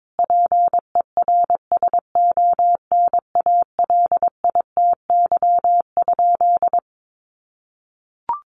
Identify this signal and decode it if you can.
Morse code, heard as a keyed tone, about 22 wpm, 700 Hz. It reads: PERSONALITY?
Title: Sentences from Top 1400 Words 22wpm